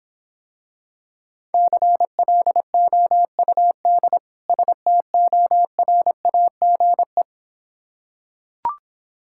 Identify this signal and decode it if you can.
Morse code, heard as a keyed tone, about 26 wpm, 700 Hz. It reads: CLOUD STORAGE